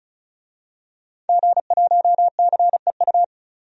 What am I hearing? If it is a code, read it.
Morse code, heard as a keyed tone, about 34 wpm, 700 Hz. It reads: G1CEU
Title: DXCC-100 Call Signs - 5 Characters - 35wpm